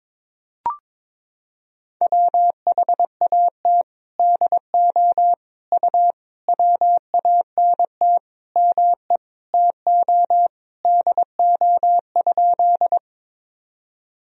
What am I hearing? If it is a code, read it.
Morse code, heard as a keyed tone, about 21 wpm, 700 Hz. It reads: WHAT DO U WANT ME TO DO?
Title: Sentences from Top 200 Words and Abbreviations 22wpm